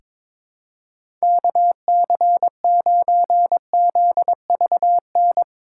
Morse code, heard as a keyed tone, about 23 wpm, 700 Hz. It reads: KC9ZVN
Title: USA Call Signs - 6 Characters - 3x Character Spacing - 22wpm